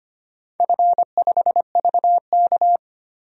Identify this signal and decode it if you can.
Morse code, heard as a keyed tone, about 25 wpm, 700 Hz. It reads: F5VK
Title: DXCC-100 Call Signs - 4 Characters - 25wpm